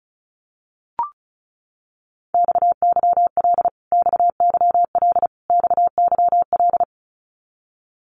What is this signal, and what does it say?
Morse code, heard as a keyed tone, about 34 wpm, 700 Hz. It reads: XYL XYL XYL
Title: Abbreviations Repeated 3x - 35wpm